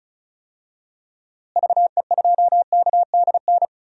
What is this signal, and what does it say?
Morse code, heard as a keyed tone, about 36 wpm, 700 Hz. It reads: VE2KDN
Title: DXCC-100 Call Signs - 6 Characters - Repeated 3x - 35wpm